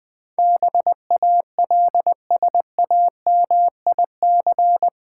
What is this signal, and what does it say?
Morse code, heard as a keyed tone, about 20 wpm, 690 Hz. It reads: BALSAMIC
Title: Difficult Words with Hints - 20wpm